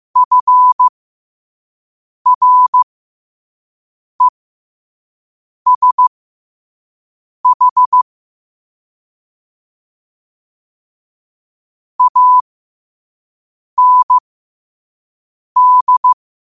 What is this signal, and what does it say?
Morse code, heard as a keyed tone, about 15 wpm, spaced as for 5 wpm, 1000 Hz. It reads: FRESH AND
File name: qotdfiles / 5wpm